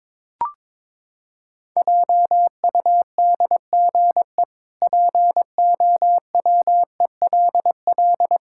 Morse code, heard as a keyed tone, about 22 wpm, 700 Hz. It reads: JUDGE POWELL
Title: Pangram Challenge v2 - 22wpm